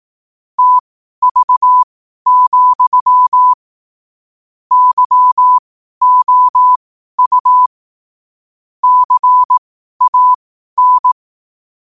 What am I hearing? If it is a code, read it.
Morse code, heard as a keyed tone, about 18 wpm, spaced as for 12 wpm, 1000 Hz. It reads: TV, YOU CAN
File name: qotdfiles / 12wpm